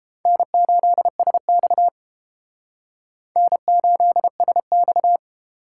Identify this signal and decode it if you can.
Morse code, heard as a keyed tone, about 32 wpm, 700 Hz. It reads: N8SX N8SX